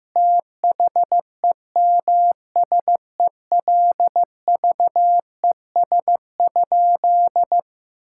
Morse code, heard as a keyed tone, about 15 wpm, 700 Hz. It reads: THEMSELVES?
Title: Sentences from Top 1600 Words 15wpm